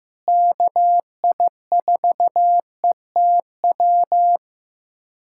Morse code, heard as a keyed tone, about 16 wpm, 700 Hz. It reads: KI4ETW
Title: USA Callsign DE Callsign - 6 Characters - 15wpm